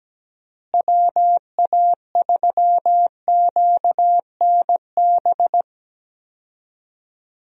Morse code, heard as a keyed tone, about 17 wpm, 700 Hz. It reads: WA3QNB
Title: USA Call Signs - 6 Characters - Repeated 2x - 17wpm